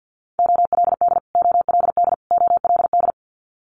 Morse code, heard as a keyed tone, about 50 wpm, 700 Hz. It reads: OLD OLD OLD
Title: Multi-letter Phonemes Repeated 3x 50wpm